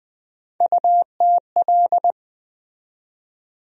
Morse code, heard as a keyed tone, about 20 wpm, 700 Hz. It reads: UTL
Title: USA Call Sign Postfixes - 3 Letters 20wpm